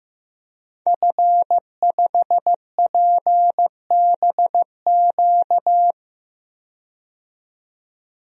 Morse code, heard as a keyed tone, about 15 wpm, 700 Hz. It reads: F5PBQ
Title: DXCC-100 Call Signs - 5 Characters - 15wpm